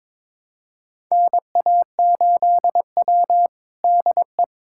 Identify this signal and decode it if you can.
Morse code, heard as a keyed tone, about 22 wpm, 700 Hz. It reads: NA8W DE
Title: USA Callsign DE Callsign - 4 Characters - 22wpm